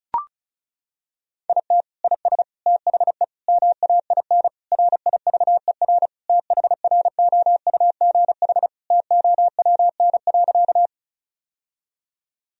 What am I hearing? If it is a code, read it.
Morse code, heard as a keyed tone, about 34 wpm, 700 Hz. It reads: IT IS THE MAIN RIVER THROUGH TOWN.
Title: Sentences from Top 1000 Words - Review Part 2 - 35wpm